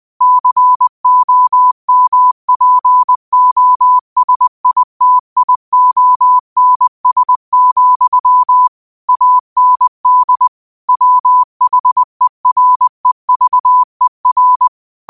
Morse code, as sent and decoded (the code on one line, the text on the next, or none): -.-. --- -- .--. --- ... .. - .. --- -. ... --..-- / .- -. -.. / .-- .... .-. ...- . .-.
COMPOSITIONS, AND WHEREVER